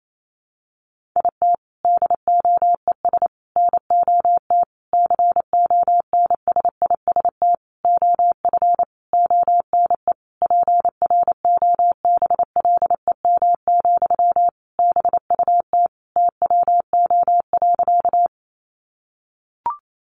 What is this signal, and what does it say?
Morse code, heard as a keyed tone, about 28 wpm, 700 Hz. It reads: IT DOES NOT CONSIST OF ONE PROBLEM, BUT TWO.
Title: Sentences from Top 1400 Words 28wpm